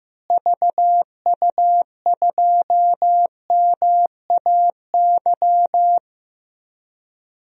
Morse code, heard as a keyed tone, about 16 wpm, 700 Hz. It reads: VU2MAY
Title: DXCC-100 Call Signs - 6 Characters - Repeated 2x - 15wpm